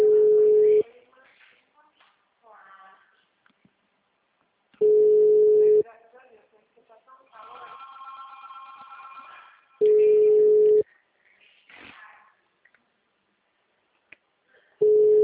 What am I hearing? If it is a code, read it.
Ringback tone (Europe, 425 Hz), with speech.